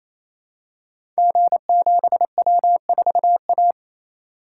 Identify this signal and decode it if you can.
Morse code, heard as a keyed tone, about 28 wpm, 700 Hz. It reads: G7W4A
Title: Code Groups - 5 Letter:Numbers - 28wpm